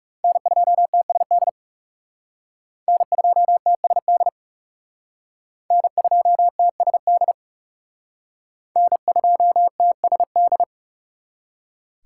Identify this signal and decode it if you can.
Morse code, heard as a keyed tone, about 36 wpm, 700 Hz. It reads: N2TSD N2TSD N2TSD N2TSD
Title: Speed Racing - USA 1x3 Call Signs 30wpm